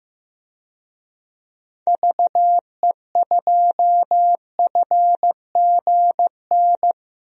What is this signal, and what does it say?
Morse code, heard as a keyed tone, about 15 wpm, 700 Hz. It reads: VE2FGN
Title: DXCC-100 Call Signs - 6 Characters - 15wpm